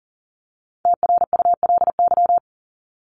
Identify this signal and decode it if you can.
Morse code, heard as a keyed tone, about 40 wpm, 700 Hz. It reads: TRULY
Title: LY Suffix - Top 100 Words with Base Words 40wpm